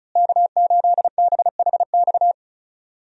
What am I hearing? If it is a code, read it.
Morse code, heard as a keyed tone, about 36 wpm, 700 Hz. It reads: K8BHX
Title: USA Call Signs - 5 Characters - Repeated 3x - 35wpm